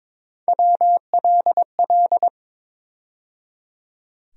Morse code, heard as a keyed tone, about 22 wpm, 700 Hz. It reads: WLL
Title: USA Call Sign Postfixes - 3 Letters 22wpm